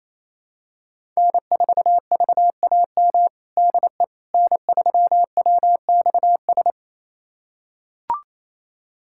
Morse code, heard as a keyed tone, about 28 wpm, 700 Hz. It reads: N4VAM DE N3WXS
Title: USA Callsign DE Callsign - 5 Characters - 28wpm